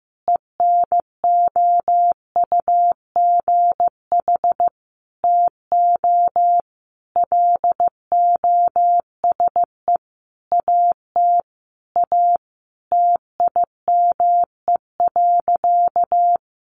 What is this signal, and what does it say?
Morse code, heard as a keyed tone, about 15 wpm, 700 Hz. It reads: ENOUGH TO LOSE AT A TIME.
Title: Sentences from Top 1100 Words 15wpm